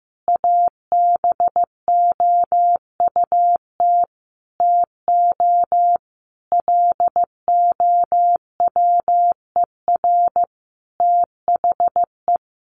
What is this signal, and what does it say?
Morse code, heard as a keyed tone, about 15 wpm, 700 Hz. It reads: ABOUT TO LOWER THE
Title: Sentences from Top 700 Words 15wpm